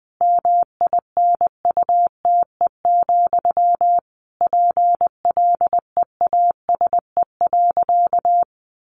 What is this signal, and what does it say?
Morse code, heard as a keyed tone, about 19 wpm, 700 Hz. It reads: MINUTE, PLEASE.
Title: Sentences from Top 1500 Words 20wpm